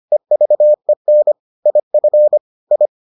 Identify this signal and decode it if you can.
Morse code, heard as a keyed tone, about 25 wpm, 600 Hz. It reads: EVEN IF I